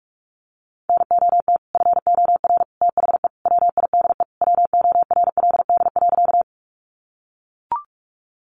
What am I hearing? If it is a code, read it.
Morse code, heard as a keyed tone, about 44 wpm, 700 Hz. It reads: NOT FOR THE WIDE WORLD.
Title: Sentences from Top 900 Words 45wpm